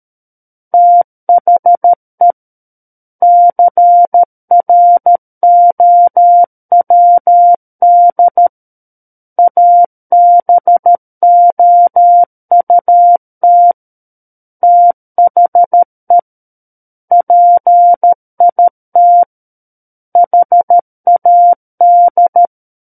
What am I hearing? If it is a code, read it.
Morse code, heard as a keyed tone, about 13 wpm, 700 Hz. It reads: THE CROWD ABOUT THE PIT HAD
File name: Book One / 04-Chapter 4 - 13 WPM